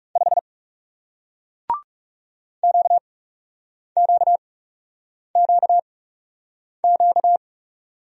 Morse code, heard as a keyed tone, about 39 wpm, spaced as for 8 wpm, 700 Hz. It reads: 5 QQQQ